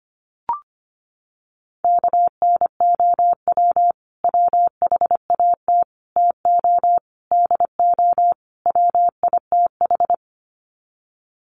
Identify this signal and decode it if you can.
Morse code, heard as a keyed tone, about 25 wpm, 700 Hz. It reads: KNOW WHAT TO DO WITH
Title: Sets of 5 Words 25wpm